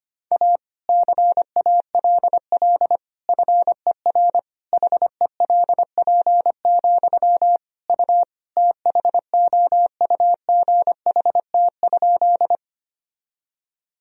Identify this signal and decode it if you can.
Morse code, heard as a keyed tone, about 25 wpm, 700 Hz. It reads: A CALL FER HELP, U THOUGHT?
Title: Sentences from Top 300 Words and Abbreviations 25wpm